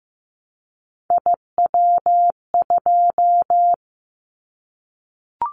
Morse code, heard as a keyed tone, about 15 wpm, 700 Hz. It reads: IW2
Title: Callsign Prefixes for Top 25 DXCC - 15wpm